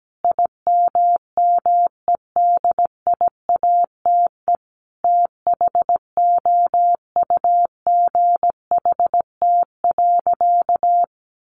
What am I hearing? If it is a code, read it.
Morse code, heard as a keyed tone, about 17 wpm, 700 Hz. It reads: IMMEDIATE THOUGHT.
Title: Sentences from Top 1500 Words 17wpm